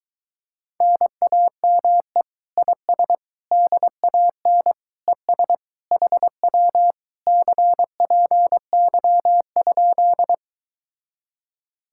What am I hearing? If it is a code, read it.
Morse code, heard as a keyed tone, about 23 wpm, 700 Hz. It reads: NAME IS DAN ES HW CPY?